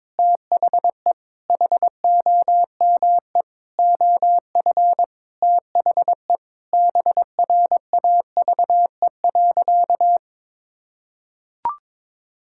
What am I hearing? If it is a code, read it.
Morse code, heard as a keyed tone, about 22 wpm, 700 Hz. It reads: THE HOME OF THE BRAVE.
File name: practice / 22wpm / Sentences with 8-Word Phrases - 22wpm